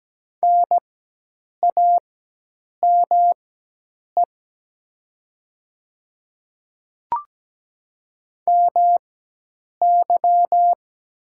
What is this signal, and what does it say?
Morse code, heard as a keyed tone, about 17 wpm, spaced as for 8 wpm, 700 Hz. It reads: NAME MY
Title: 4x Character Spacing - Top 200 Words - 17wpm